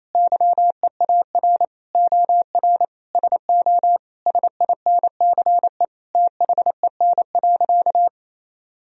Morse code, heard as a keyed tone, about 28 wpm, 700 Hz. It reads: YEAR OR SO SINCE THEN.
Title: Sentences from Top 1000 Words - Review Part 1 - 28wpm